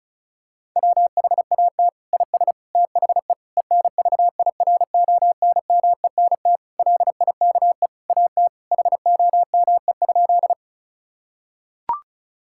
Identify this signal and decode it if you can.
Morse code, heard as a keyed tone, about 34 wpm, 700 Hz. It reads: WHAT IS THE ENVIRONMENT LIKE AT HOME?
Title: Sentences from Top 1000 Words - Review Part 5 - 35wpm